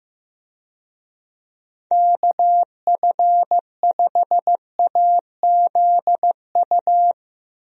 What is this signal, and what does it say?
Morse code, heard as a keyed tone, about 15 wpm, 700 Hz. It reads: KF5AZU
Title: USA Call Signs - 6 Characters - Repeated 2x - 15wpm